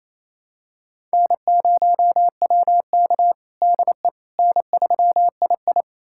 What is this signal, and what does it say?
Morse code, heard as a keyed tone, about 28 wpm, 700 Hz. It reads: N0WK DE N3II
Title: USA Callsign DE Callsign - 4 Characters - 28wpm